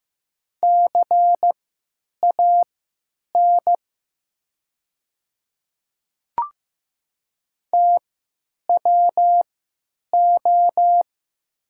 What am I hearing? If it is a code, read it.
Morse code, heard as a keyed tone, about 15 wpm, spaced as for 9 wpm, 700 Hz. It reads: CAN TWO
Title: 3x Character Spacing - Top 100 Words - 15wpm